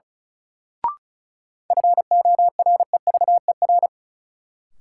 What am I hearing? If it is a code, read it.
Morse code, heard as a keyed tone, about 36 wpm, 700 Hz. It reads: FOREVER
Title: Sentences from Top 1600 Words - New Word Drill - 35wpm